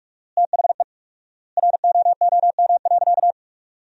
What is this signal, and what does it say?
Morse code, heard as a keyed tone, about 45 wpm, 700 Hz. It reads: THE ROOM.
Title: 4x Word-Spacing - Sentences from Top 400 Words - 45wpm